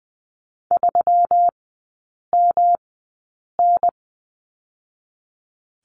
Morse code, heard as a keyed tone, about 20 wpm, spaced as for 8 wpm, 700 Hz. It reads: 3MN
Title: 121 Lesson 20e - Number 3 - Three Characters 20wpm